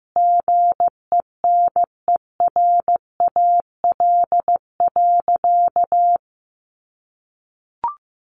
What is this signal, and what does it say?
Morse code, heard as a keyed tone, about 15 wpm, 700 Hz. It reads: GENERAL.
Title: Sentences from Top 600 Words 15wpm